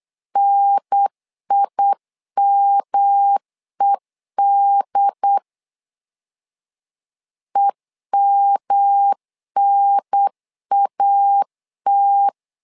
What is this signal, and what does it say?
Morse code, heard as a keyed tone, about 8 wpm, 800 Hz. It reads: NIMED EMNAT
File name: LEZIONI / 3 MWRO / 3 M 8wpm 2300